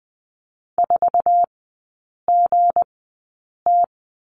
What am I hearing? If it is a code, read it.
Morse code, heard as a keyed tone, about 20 wpm, spaced as for 8 wpm, 700 Hz. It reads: 4GT
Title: 166 Lesson 27e - Number 7 - Three Characters 20wpm